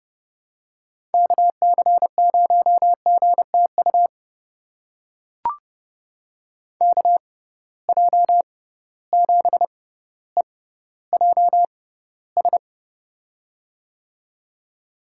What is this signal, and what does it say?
Morse code, heard as a keyed tone, about 31 wpm, spaced as for 15 wpm, 700 Hz. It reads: KC0GTU KJ7EJS
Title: USA Call Signs - 6 Characters - 6x Character Spacing - 30wpm